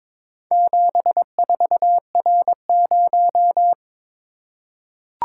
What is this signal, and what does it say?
Morse code, heard as a keyed tone, about 23 wpm, 700 Hz. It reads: 74R0
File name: practice / 22wpm / Four Letter-Number Combinations - 22wpm